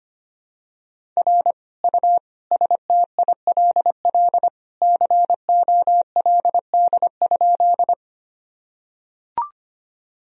Morse code, heard as a keyed tone, about 25 wpm, 700 Hz. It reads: R U STILL COLD?
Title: Sentences from Top 400 Words and Abbreviations 25wpm